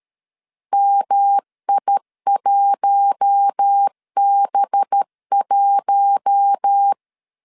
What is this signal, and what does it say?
Morse code, heard as a keyed tone, about 13 wpm, 800 Hz. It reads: MI1B1
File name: CW / DET / 1 9 1500C S A 12wpm 3800